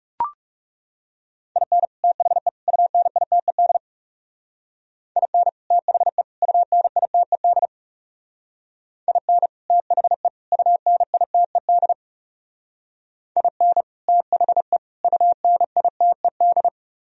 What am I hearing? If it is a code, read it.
Morse code, heard as a keyed tone, about 35 wpm, 700 Hz. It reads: IN THE UNITED IN THE UNITED IN THE UNITED IN THE UNITED
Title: Speed-Racing - Sets of 3 Words 30wpm